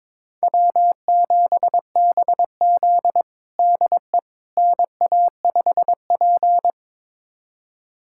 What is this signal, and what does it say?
Morse code, heard as a keyed tone, about 22 wpm, 700 Hz. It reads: W7BZ DE NA5P
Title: USA Callsign DE Callsign - 4 Characters - 22wpm